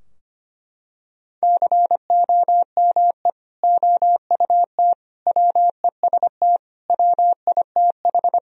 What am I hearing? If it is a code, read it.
Morse code, heard as a keyed tone, about 25 wpm, 700 Hz. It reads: COME OUT WEST WITH
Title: Sentences from Top 1000 Words - Review Part 2 - 25wpm